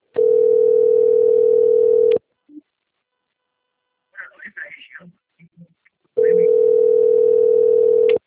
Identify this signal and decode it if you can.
Ringback tone (North America), 440 + 480 Hz, with speech.
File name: rg-1600-unknown- - - .42275